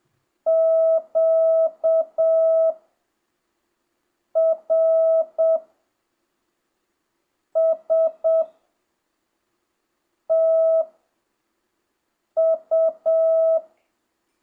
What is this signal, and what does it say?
Morse code, heard as a keyed tone, about 7 wpm, spaced as for 4 wpm, 650 Hz. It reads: QRSTU
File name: jh8cbh / 2025cw